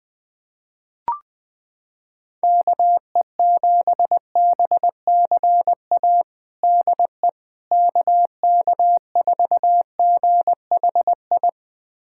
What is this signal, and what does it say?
Morse code, heard as a keyed tone, about 20 wpm, 700 Hz. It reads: KE7BCA DE KK4GHI
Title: USA Callsign DE Callsign 20wpm